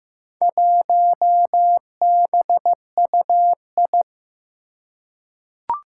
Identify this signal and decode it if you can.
Morse code, heard as a keyed tone, about 15 wpm, 700 Hz. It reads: 1BUI
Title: Four Challenging Letter-Number Combinations 15wpm